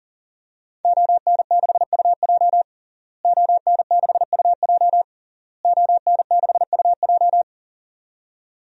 Morse code, heard as a keyed tone, about 40 wpm, 700 Hz. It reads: ON6UJ ON6UJ ON6UJ
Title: DXCC-100 Call Signs - 5 Characters - Repeated 3x - 40wpm